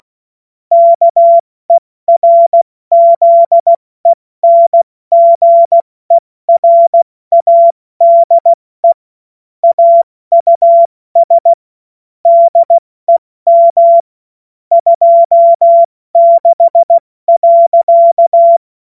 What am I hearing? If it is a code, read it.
Morse code, heard as a keyed tone, about 16 wpm, spaced as for 14 wpm, 680 Hz. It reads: KERZENGERADE AUS DEM 26.
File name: Morsekurs / Witze / Witz 02 April 2024 70 F80